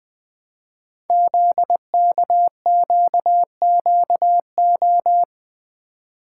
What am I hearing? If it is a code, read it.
Morse code, heard as a keyed tone, about 20 wpm, 700 Hz. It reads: ZKQQO
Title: Code Groups - 5 Letters - 20wpm